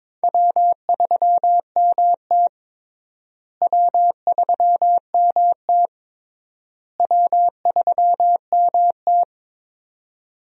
Morse code, heard as a keyed tone, about 22 wpm, 700 Hz. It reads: W3MT W3MT W3MT
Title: USA Call Signs - 4 Characters - Repeated 3x - 22wpm